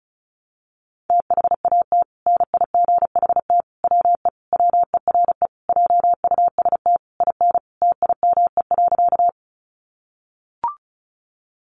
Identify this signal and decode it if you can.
Morse code, heard as a keyed tone, about 34 wpm, 700 Hz. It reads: THAT NIGHT WE WERE JUST IN TIME.
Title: Sentences from Top 300 Words 35wpm